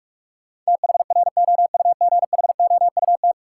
Morse code, heard as a keyed tone, about 46 wpm, 700 Hz. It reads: THROUGHOUT
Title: Sentences from Top 800 Words - New Word Drill - 45wpm